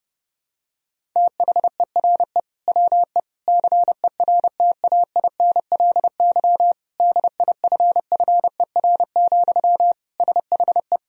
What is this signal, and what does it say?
Morse code, heard as a keyed tone, about 30 wpm, 700 Hz. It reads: THERE WE CERTAINLY DIFFER, SHE
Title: Sentences from Top 1000 Words - Review Part 1 - 30wpm